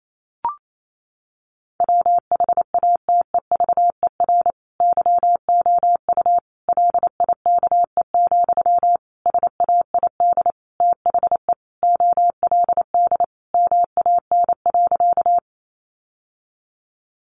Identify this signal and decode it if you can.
Morse code, heard as a keyed tone, about 28 wpm, 700 Hz. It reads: WHATEVER YOU LIKE, SAID THE OLD MAN.
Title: Sentences from Top 1100 Words 28wpm